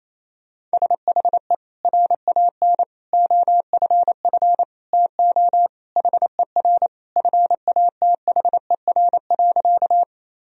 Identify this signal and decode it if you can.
Morse code, heard as a keyed tone, about 28 wpm, 700 Hz. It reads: SHE RAN OFF TO HER FATHER.
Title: Sentences from Top 1000 Words - Review Part 3 - 28wpm